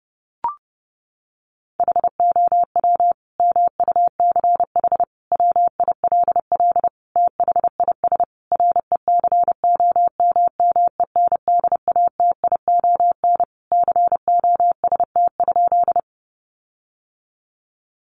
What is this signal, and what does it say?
Morse code, heard as a keyed tone, about 30 wpm, 700 Hz. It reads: HOW MUCH WILL THIS RECOMMENDATION COST?
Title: Sentences from Top 1400 Words 30wpm